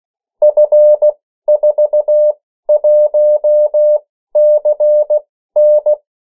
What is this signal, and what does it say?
Morse code, heard as a keyed tone, about 16 wpm, spaced as for 13 wpm, 600 Hz. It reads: F41CN
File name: morse mp3 / 14 urok 13